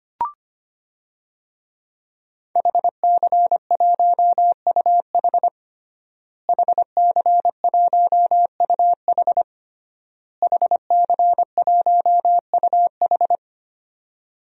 Morse code, heard as a keyed tone, about 25 wpm, 700 Hz. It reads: HC1UH HC1UH HC1UH
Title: DXCC-100 Call Signs - 5 Characters - Repeated 3x - 25wpm